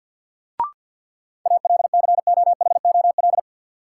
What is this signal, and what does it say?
Morse code, heard as a keyed tone, about 50 wpm, 700 Hz. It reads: ALCOHOL
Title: Sentences from Top 1400 Words - New Word Drill Alt - 50wpm